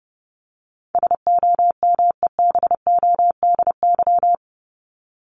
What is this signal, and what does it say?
Morse code, heard as a keyed tone, about 30 wpm, 700 Hz. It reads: SOMEBODY
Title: Top 1751-2000 Words 30wpm